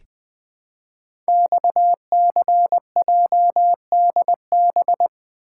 Morse code, heard as a keyed tone, about 20 wpm, 700 Hz. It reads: XCJDB
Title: Code Groups - 5 Letters - 20wpm